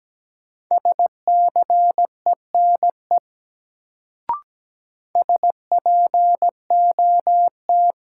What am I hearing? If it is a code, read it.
Morse code, heard as a keyed tone, about 17 wpm, 700 Hz. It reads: SCENE SPOT
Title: Sentences from Top 1100 Words - New Word Drill Alt - 17wpm